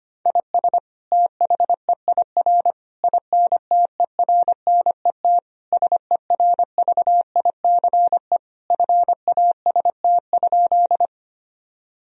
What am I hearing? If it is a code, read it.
Morse code, heard as a keyed tone, about 25 wpm, 700 Hz. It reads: IS THEIR INTERNET SERVICE FAST?
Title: Sentences from Top 1000 Words - Review Part 4 - 25wpm